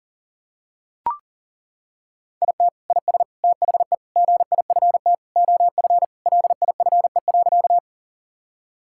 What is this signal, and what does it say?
Morse code, heard as a keyed tone, about 40 wpm, 700 Hz. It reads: IT IS THE GIFT OF LIFE.
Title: Sentences from Top 1000 Words - Review Part 6 - 40wpm